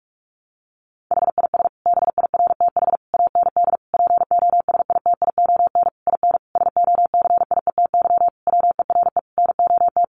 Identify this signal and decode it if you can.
Morse code, heard as a keyed tone, about 44 wpm, 700 Hz. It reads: HIS BIRTH AND POSITION IN SOCIETY WERE NOT
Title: Sentences from Top 900 Words 45wpm